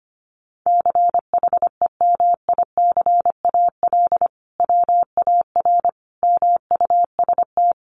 Morse code, heard as a keyed tone, about 25 wpm, 700 Hz. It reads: CHEMICAL WAR MUST
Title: Sentences from Top 1200 Words 25wpm